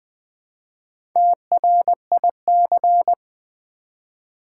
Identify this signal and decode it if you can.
Morse code, heard as a keyed tone, about 20 wpm, 700 Hz. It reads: TRIC
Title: Build A Word - 1 Letter At A Time - 5-Letter Words - 20wpm